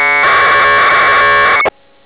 Packet radio (AX.25, as in APRS).